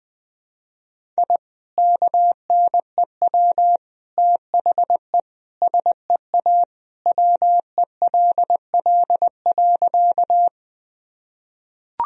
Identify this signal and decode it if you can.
Morse code, heard as a keyed tone, about 20 wpm, 700 Hz. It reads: I KNEW THE SEA WELL.
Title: Sentences from Top 1000 Words - Review Part 1 - 20wpm